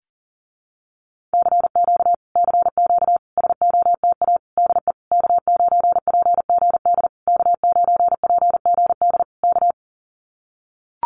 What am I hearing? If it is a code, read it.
Morse code, heard as a keyed tone, about 40 wpm, 700 Hz. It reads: CQ CQ SOTA DE K9PGD K9PGD K
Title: Common QSO Phrases Encore 40wpm